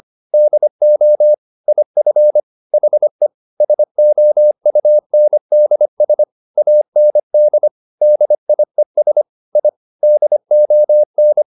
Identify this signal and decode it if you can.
Morse code, heard as a keyed tone, about 25 wpm, 600 Hz. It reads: DO IF HE SOUNDS AND DIES I DON
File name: the old man and the sea part 1 25wpm